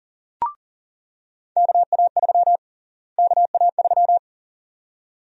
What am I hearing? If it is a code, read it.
Morse code, heard as a keyed tone, about 40 wpm, 700 Hz. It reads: KA3 KA3
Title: USA Call Sign Prefixes - Repeated 2x - 40wpm